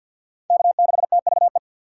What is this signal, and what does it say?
Morse code, heard as a keyed tone, about 46 wpm, 700 Hz. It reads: K6TVE
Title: USA Call Signs - 5 Characters - 2x Character Spacing - 50wpm